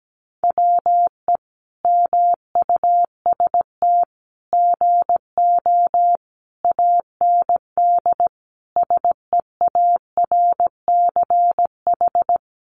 Morse code, heard as a keyed tone, about 17 wpm, 700 Hz. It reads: WE MUST GO AND SEARCH
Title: Sentences from Top 800 Words 17wpm